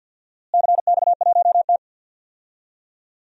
Morse code, heard as a keyed tone, about 46 wpm, 700 Hz. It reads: CX1T
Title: DXCC-100 Call Signs - 4 Characters - Repeated 2x - 50wpm